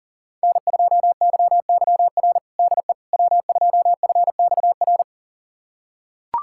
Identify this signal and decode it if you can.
Morse code, heard as a keyed tone, about 40 wpm, 700 Hz. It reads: N2YYR DE W2FXR
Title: USA Callsign DE Callsign - 5 Characters - 40wpm